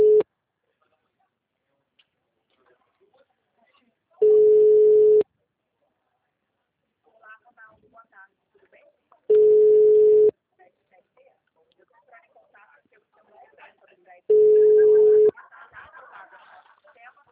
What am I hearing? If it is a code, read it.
Ringback tone (Europe, 425 Hz), with speech.
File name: out- -2017- - - .76346